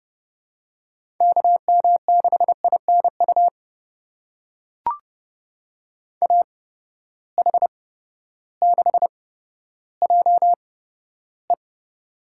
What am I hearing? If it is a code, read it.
Morse code, heard as a keyed tone, about 30 wpm, spaced as for 14 wpm, 700 Hz. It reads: KM6INU AH6JE
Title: USA Call Signs - 8x Character Spacing - 30wpm